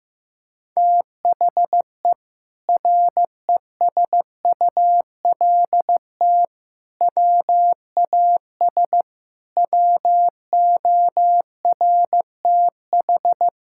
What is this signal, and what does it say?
Morse code, heard as a keyed tone, about 15 wpm, 700 Hz. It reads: THE RESULT WAS WORTH
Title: Sentences from Top 1000 Words - Review Part 5 - 15wpm